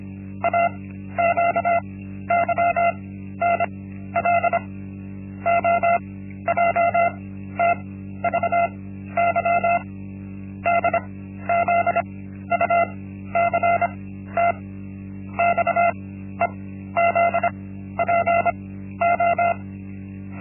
Morse code, heard as a keyed tone, about 26 wpm, spaced as for 13 wpm, 690 Hz. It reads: AQYNL OJTVY DZUCT XEZPO
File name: Linksmakalnis-1